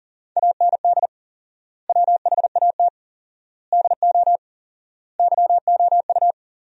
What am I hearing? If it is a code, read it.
Morse code, heard as a keyed tone, about 40 wpm, 700 Hz. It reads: AND WHAT DO YOU
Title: 4x Word-Spacing - Sentences from Top 200 Words - 40wpm